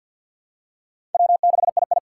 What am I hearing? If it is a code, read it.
Morse code, heard as a keyed tone, about 51 wpm, 700 Hz. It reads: W6II
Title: USA Call Signs - 4 Characters - 3x Character Spacing - 50wpm